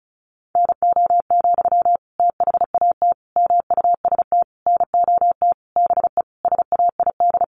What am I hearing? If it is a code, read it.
Morse code, heard as a keyed tone, about 34 wpm, 700 Hz. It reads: NO, THAT MUST NOT BE SAID
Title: Sentences from Top 600 Words 35wpm